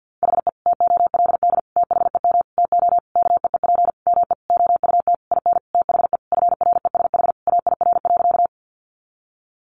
Morse code, heard as a keyed tone, about 50 wpm, 700 Hz. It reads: HE TOLD THEM TO KEEP ME OUT IN THE FRESH AIR.
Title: Sentences from Top 1200 Words 50wpm